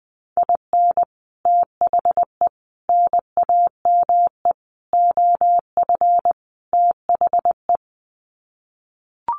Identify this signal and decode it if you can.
Morse code, heard as a keyed tone, about 20 wpm, 700 Hz. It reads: IN THE NAME OF THE
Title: Sets of 5 Words Encore 20wpm